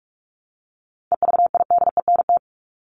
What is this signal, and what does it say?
Morse code, heard as a keyed tone, about 44 wpm, 700 Hz. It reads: EVIDENT
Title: LY Suffix - Top 100 Words with Base Words 45wpm